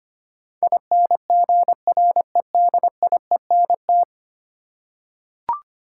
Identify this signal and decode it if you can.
Morse code, heard as a keyed tone, about 25 wpm, 700 Hz. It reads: INGREDIENT
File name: Ten Letter Words - 25wpm